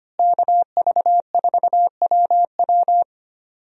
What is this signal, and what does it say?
Morse code, heard as a keyed tone, about 25 wpm, 700 Hz. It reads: KV4WW
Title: USA Call Signs - 5 Characters - Repeated 2x - 25wpm